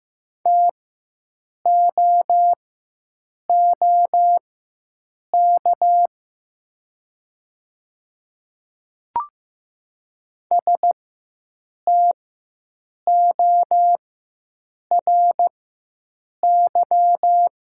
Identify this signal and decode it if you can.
Morse code, heard as a keyed tone, about 15 wpm, spaced as for 7 wpm, 700 Hz. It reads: TOOK STORY
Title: 4x Character Spacing - Top 300 Words - 15wpm